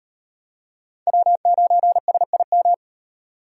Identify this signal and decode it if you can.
Morse code, heard as a keyed tone, about 38 wpm, 700 Hz. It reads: W9SIM